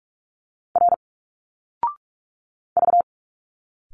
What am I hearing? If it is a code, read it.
Morse code, heard as a keyed tone, about 43 wpm, 700 Hz. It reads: R V
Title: Single Letters 45wpm